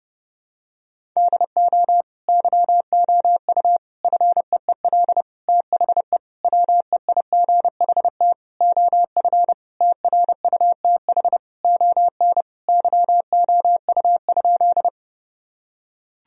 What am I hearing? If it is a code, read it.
Morse code, heard as a keyed tone, about 30 wpm, 700 Hz. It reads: DO YOU FEEL THE WEIGHT OF TRUTH ON YOU?
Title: Sentences from Top 1000 Words - Review Part 3 - 30wpm